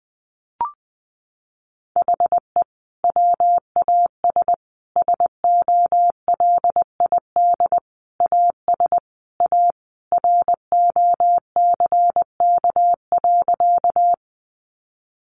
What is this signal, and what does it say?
Morse code, heard as a keyed tone, about 20 wpm, 700 Hz. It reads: HE WAS SOLID AS A ROCK.
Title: Sentences from Top 1500 Words 20wpm